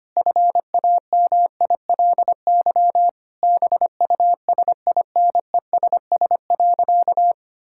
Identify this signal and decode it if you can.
Morse code, heard as a keyed tone, about 25 wpm, 700 Hz. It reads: FAMILY BUSINESS.
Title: Sentences from Top 1000 Words - Review Part 5 - 25wpm